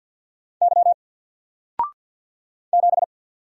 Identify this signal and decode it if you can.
Morse code, heard as a keyed tone, about 49 wpm, 700 Hz. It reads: Y 7
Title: Single Character (Letter:Number:Punctuation:Prosign) 50wpm